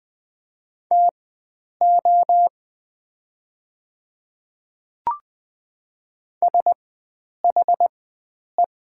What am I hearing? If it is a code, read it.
Morse code, heard as a keyed tone, about 20 wpm, spaced as for 9 wpm, 700 Hz. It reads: TO SHE
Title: 4x Character Spacing - Top 100 Words - 20wpm